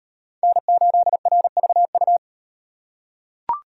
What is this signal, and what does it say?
Morse code, heard as a keyed tone, about 36 wpm, 700 Hz. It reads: N8RVU